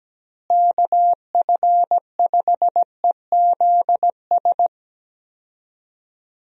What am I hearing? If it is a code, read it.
Morse code, heard as a keyed tone, about 17 wpm, 700 Hz. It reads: KF5EZS
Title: USA Call Signs - 6 Characters - Repeated 3x - 17wpm